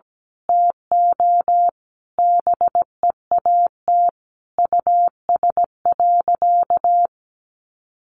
Morse code, heard as a keyed tone, about 17 wpm, 700 Hz. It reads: TO BEAT US.
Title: Sentences from Top 1300 Words 17wpm